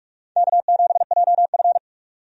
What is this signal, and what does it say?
Morse code, heard as a keyed tone, about 46 wpm, 700 Hz. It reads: K7JF
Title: USA Call Signs - 4 Characters - 45wpm